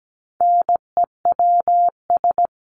Morse code, heard as a keyed tone, about 17 wpm, 700 Hz. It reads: NEWS
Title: Top 751-1000 Words 17wpm